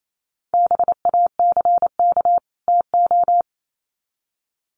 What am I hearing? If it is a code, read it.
Morse code, heard as a keyed tone, about 28 wpm, 700 Hz. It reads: BACK TO
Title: Sets of 2 Words 28wpm